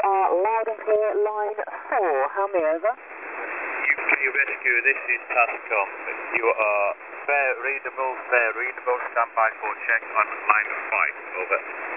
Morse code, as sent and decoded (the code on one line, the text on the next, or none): none